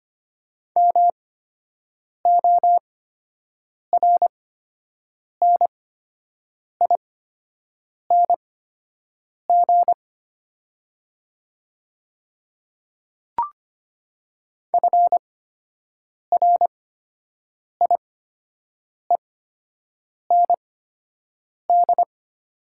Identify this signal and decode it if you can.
Morse code, heard as a keyed tone, about 25 wpm, spaced as for 7 wpm, 700 Hz. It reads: MORNING FRIEND
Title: 8x Character Spacing - Top 500 Words - 25wpm